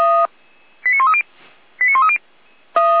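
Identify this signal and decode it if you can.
Five-tone selective call (ZVEI), heard as quick runs of five tones, tones of about 70 ms. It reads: 89130 89130
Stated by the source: Five-tone ZVEI selective call 89135, received by SDR.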